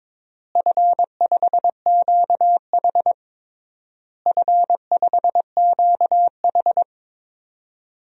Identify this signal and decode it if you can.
Morse code, heard as a keyed tone, about 22 wpm, 700 Hz. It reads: F5QH F5QH
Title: DXCC-100 Call Signs - 4 Characters - Repeated 3x - 22wpm